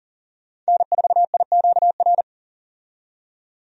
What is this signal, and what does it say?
Morse code, heard as a keyed tone, about 37 wpm, 700 Hz. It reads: N4IQR
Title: USA Call Signs - 5 Characters - 40wpm